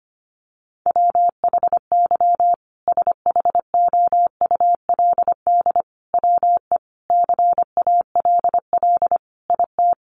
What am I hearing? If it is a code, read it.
Morse code, heard as a keyed tone, about 25 wpm, 700 Hz. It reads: WHY SHOULD WE CALL IT
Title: Sentences from Top 1100 Words 25wpm